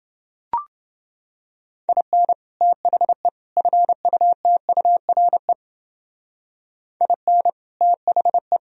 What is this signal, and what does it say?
Morse code, heard as a keyed tone, about 29 wpm, 700 Hz. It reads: IN THE FUTURE IN THE